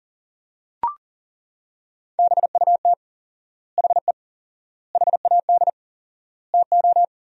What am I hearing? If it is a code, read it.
Morse code, heard as a keyed tone, about 40 wpm, 700 Hz. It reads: BUT HE HAD TO
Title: 4x Word-Spacing - Sentences from Top 100 Words - 40wpm